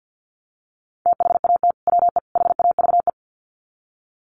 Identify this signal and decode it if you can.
Morse code, heard as a keyed tone, about 48 wpm, 700 Hz. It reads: THAT WE HAVE
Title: Sets of 3 Words 50wpm